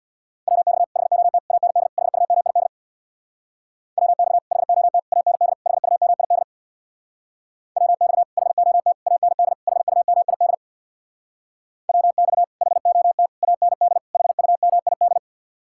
Morse code, heard as a keyed tone, about 66 wpm, 700 Hz. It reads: WX HOT AND HUMID WX HOT AND HUMID WX HOT AND HUMID WX HOT AND HUMID